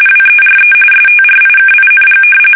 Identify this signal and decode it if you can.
RTTY, about 45.45 baud. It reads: WERE STRIKING
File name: fsk850 / battleshipnj